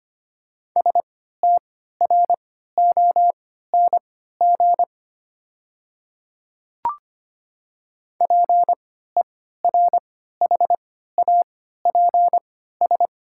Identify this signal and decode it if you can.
Morse code, heard as a keyed tone, about 25 wpm, spaced as for 14 wpm, 700 Hz. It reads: STRONG PERHAPS
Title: 3x Character Spacing - Top 500 Words - 25wpm